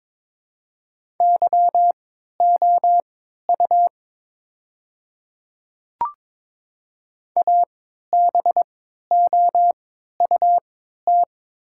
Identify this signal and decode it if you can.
Morse code, heard as a keyed tone, about 22 wpm, spaced as for 13 wpm, 700 Hz. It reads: YOU ABOUT
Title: 3x Character Spacing - Top 200 Words - 22wpm